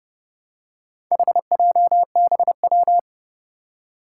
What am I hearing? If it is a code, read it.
Morse code, heard as a keyed tone, about 30 wpm, 700 Hz. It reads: HJBW
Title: Four Challenging Letter-Number Combinations 30wpm